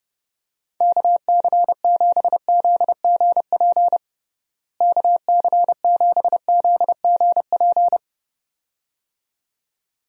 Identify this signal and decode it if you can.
Morse code, heard as a keyed tone, about 32 wpm, 700 Hz. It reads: KC7ZGP KC7ZGP
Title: USA Call Signs - 6 Characters - Repeated 2x - 30wpm